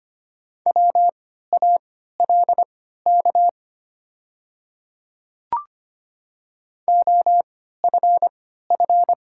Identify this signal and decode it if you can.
Morse code, heard as a keyed tone, about 25 wpm, spaced as for 14 wpm, 700 Hz. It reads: WALK OFF
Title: 3x Character Spacing - Top 300 Words - 25wpm